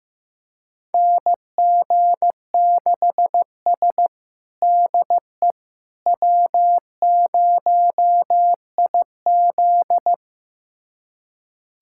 Morse code, heard as a keyed tone, about 15 wpm, 700 Hz. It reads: NG6S DE W0IZ
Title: USA Callsign DE Callsign - 4 Characters - 15wpm